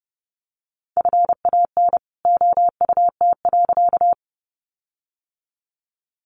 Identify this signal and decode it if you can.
Morse code, heard as a keyed tone, about 30 wpm, 700 Hz. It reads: FAN OUT.
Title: Sentences from Top 1300 Words 30wpm